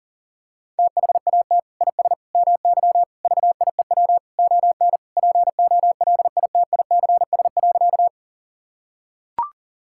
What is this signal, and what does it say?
Morse code, heard as a keyed tone, about 40 wpm, 700 Hz. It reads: THAT IS MY VIEW ON POLITICS.
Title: Sentences from Top 1000 Words - Review Part 6 - 40wpm